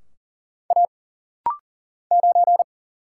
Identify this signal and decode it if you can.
Morse code, heard as a keyed tone, about 40 wpm, 700 Hz. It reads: A 9
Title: Single Letter-Number - Rapid-Fire 40wpm